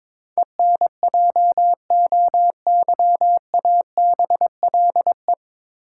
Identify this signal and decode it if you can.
Morse code, heard as a keyed tone, about 21 wpm, 700 Hz. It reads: ENJOYABLE
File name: 9 Letter Words - First 3 Letters Given - 22wpm